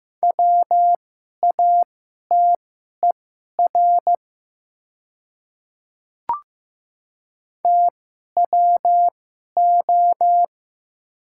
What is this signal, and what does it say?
Morse code, heard as a keyed tone, about 15 wpm, spaced as for 11 wpm, 700 Hz. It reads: WATER TWO
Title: 2x Character Spacing - Top 100 Words - 15wpm